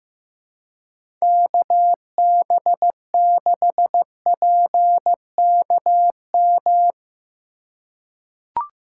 Morse code, heard as a keyed tone, about 15 wpm, 700 Hz. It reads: KB6PKM
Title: USA Call Signs - 6 Characters - 4x Character Spacing - 15wpm